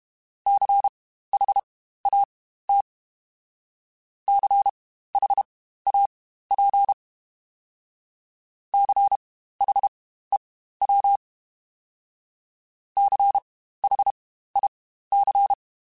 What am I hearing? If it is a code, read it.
Morse code, heard as a keyed tone, about 32 wpm, spaced as for 15 wpm, 800 Hz. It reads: CHAT CHAP CHEW CHIC
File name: Spaced Alphas / Spaced Alphas mp3s / Cs sp @30